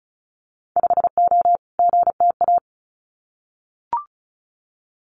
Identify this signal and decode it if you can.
Morse code, heard as a keyed tone, about 36 wpm, 700 Hz. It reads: 5O GTA
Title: Winter Field Day Exchanges Encore 35wpm